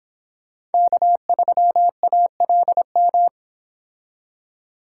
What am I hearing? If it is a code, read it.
Morse code, heard as a keyed tone, about 29 wpm, 700 Hz. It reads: K3ALM